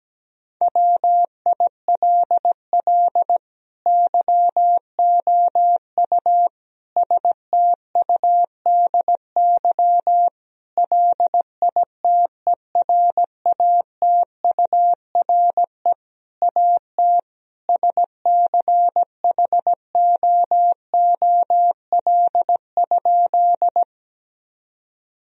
Morse code, heard as a keyed tone, about 17 wpm, 700 Hz. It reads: WILL YOU STUDY LITERATURE AT SCHOOL?
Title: Sentences from Top 1000 Words - Review Part 3 - 17wpm